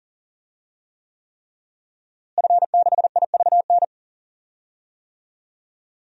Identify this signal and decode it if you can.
Morse code, heard as a keyed tone, about 40 wpm, 700 Hz. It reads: F6IVN
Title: DXCC-100 Call Signs - 5 Characters - Repeated 3x - 40wpm